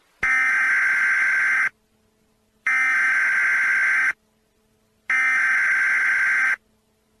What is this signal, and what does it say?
Emergency Alert System (SAME) tones (header bursts), heard as harsh buzzing bursts. It is heard from 0.2 to 6.5 s.